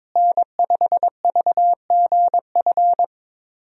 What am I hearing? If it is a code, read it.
Morse code, heard as a keyed tone, about 22 wpm, 700 Hz. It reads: N5VGF
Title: USA Call Signs - 5 Characters - Repeated 3x - 22wpm